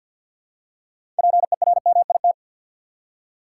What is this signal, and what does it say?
Morse code, heard as a keyed tone, about 52 wpm, 700 Hz. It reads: PERMIT
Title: Six Letter Words - 50wpm